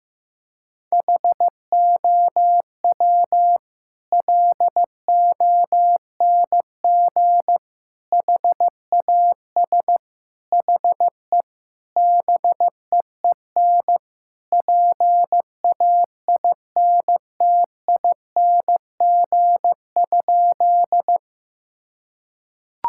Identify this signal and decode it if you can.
Morse code, heard as a keyed tone, about 15 wpm, 700 Hz. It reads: HOW LONG HAS HE BEEN PAINTING?
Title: Sentences from Top 1000 Words - Review Part 6 - 15wpm